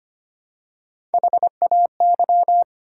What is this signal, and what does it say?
Morse code, heard as a keyed tone, about 25 wpm, 690 Hz. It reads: HAY
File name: practice / 25wpm / Easy Words with Hints - 25wpm